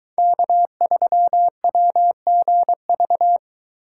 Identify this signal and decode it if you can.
Morse code, heard as a keyed tone, about 23 wpm, 700 Hz. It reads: K3WGV